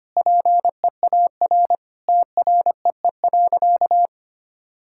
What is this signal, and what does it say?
Morse code, heard as a keyed tone, about 25 wpm, 700 Hz. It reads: PEAR TREE.
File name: practice / 25wpm / Sentences with 9-Word Phrases - 25wpm